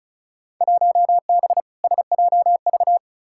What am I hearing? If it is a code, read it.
Morse code, heard as a keyed tone, about 36 wpm, 700 Hz. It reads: 1B SJV
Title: Field Day Exchanges - 35wpm